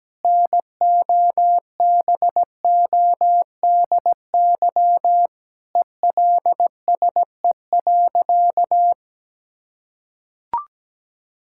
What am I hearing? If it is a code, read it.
Morse code, heard as a keyed tone, about 17 wpm, 700 Hz. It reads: NOBODY ELSE.
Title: Sentences from Top 1600 Words 17wpm